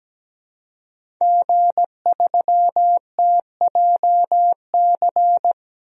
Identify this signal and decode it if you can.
Morse code, heard as a keyed tone, about 17 wpm, 700 Hz. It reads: G3TJC
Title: DXCC-100 Call Signs - 5 Characters - 17wpm